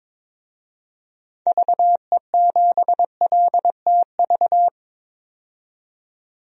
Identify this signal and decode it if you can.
Morse code, heard as a keyed tone, about 22 wpm, 700 Hz. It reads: VE7LTV
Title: DXCC-100 Call Signs - 6 Characters - 22wpm